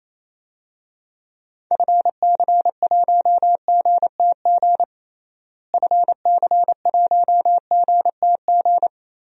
Morse code, heard as a keyed tone, about 27 wpm, 700 Hz. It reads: FC1GTG FC1GTG
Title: DXCC-100 Call Signs - 6 Characters - Repeated 3x - 28wpm